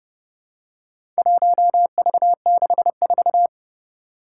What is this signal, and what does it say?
Morse code, heard as a keyed tone, about 30 wpm, 700 Hz. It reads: 1V64
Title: Four Challenging Letter-Number Combinations 30wpm